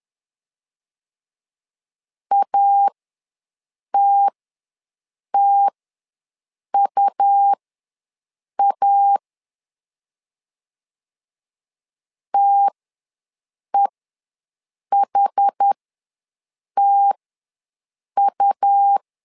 Morse code, heard as a keyed tone, about 10 wpm, spaced as for 6 wpm, 800 Hz. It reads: ATTUA TEHTU
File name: LEZIONI / 2 UHDN / 2 UH ws 10wpm 2029